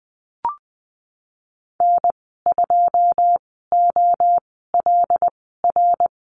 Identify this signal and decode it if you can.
Morse code, heard as a keyed tone, about 20 wpm, spaced as for 15 wpm, 700 Hz. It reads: N2OLR
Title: 179 Lesson 29g - Slash - Call Signs 20wpm